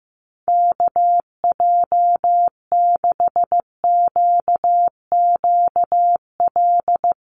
Morse code, heard as a keyed tone, about 15 wpm, 700 Hz. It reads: KJ6QQL
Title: USA Call Signs Repeated 3x 15wpm